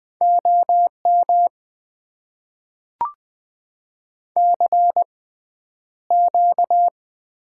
Morse code, heard as a keyed tone, about 20 wpm, spaced as for 10 wpm, 700 Hz. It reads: OM CQ
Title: Top 90 QSO Elements - 6x Character Spacing - 20wpm